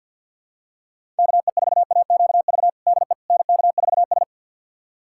Morse code, heard as a keyed tone, about 51 wpm, 700 Hz. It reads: KE4AQV DE NK4S
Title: USA Callsign DE Callsign 50wpm